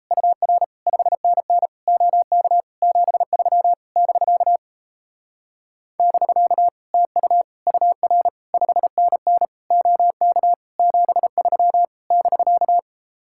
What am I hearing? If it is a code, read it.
Morse code, heard as a keyed tone, about 37 wpm, 700 Hz. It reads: UR 5NN OK 73 <BK> <BK> TU UR 5NN OK 73 <BK>